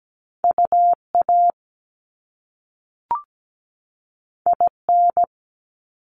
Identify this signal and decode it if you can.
Morse code, heard as a keyed tone, about 17 wpm, 700 Hz. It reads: UA IN
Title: Top Two Letter Combinations in Words 17wpm